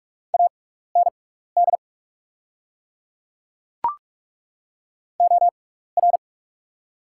Morse code, heard as a keyed tone, about 42 wpm, spaced as for 15 wpm, 700 Hz. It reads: AND OR
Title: 6x Character Spacing - Top 300 Words - 45wpm